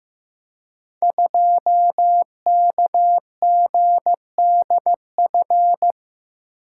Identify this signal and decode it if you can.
Morse code, heard as a keyed tone, about 15 wpm, 700 Hz. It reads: 2KGDF
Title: Code Groups - 5 Letter:Numbers - 15wpm